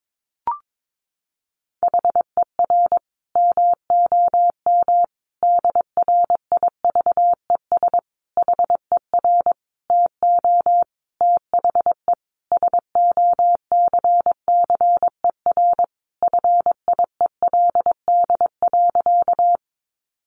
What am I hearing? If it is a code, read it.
Morse code, heard as a keyed tone, about 22 wpm, 700 Hz. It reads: HER MOM DRIVES HER TO THE SOCCER FIELD.
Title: Easy Phrases 22wpm